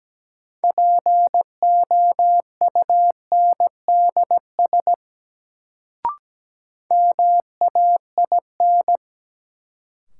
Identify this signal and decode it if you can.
Morse code, heard as a keyed tone, about 17 wpm, 700 Hz. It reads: POUNDS MAIN
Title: Sentences from Top 1000 Words - Review Part 2 - New Word Drill - 17wpm